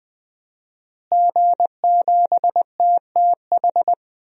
Morse code, heard as a keyed tone, about 20 wpm, 700 Hz. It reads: G7TTH
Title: DXCC-100 Call Signs - 5 Characters - Repeated 3x - 20wpm